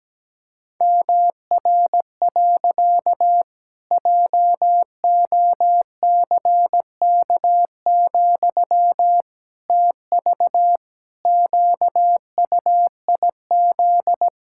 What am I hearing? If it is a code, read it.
Morse code, heard as a keyed tone, about 17 wpm, 700 Hz. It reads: MR. JOCK, TV QUIZ
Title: Pangram Challenge v2 - 17wpm